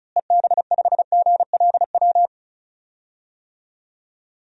Morse code, heard as a keyed tone, about 36 wpm, 700 Hz. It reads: EB5GLW
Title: DXCC-100 Call Signs - 6 Characters - Repeated 3x - 35wpm